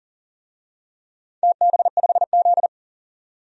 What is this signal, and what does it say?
Morse code, heard as a keyed tone, about 40 wpm, 700 Hz. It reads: TB5Z
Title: DXCC-100 Call Signs - 4 Characters - 40wpm